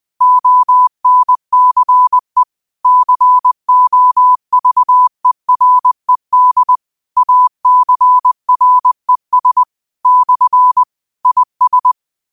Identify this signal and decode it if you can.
Morse code, heard as a keyed tone, about 20 wpm, 1000 Hz. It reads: ONCE COVERED ACRES / IS